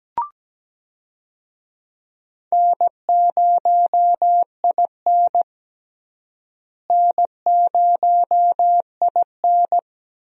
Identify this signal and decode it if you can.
Morse code, heard as a keyed tone, about 17 wpm, 700 Hz. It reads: N0IN N0IN
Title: USA Call Signs - 4 Characters - Repeated 2x - 17wpm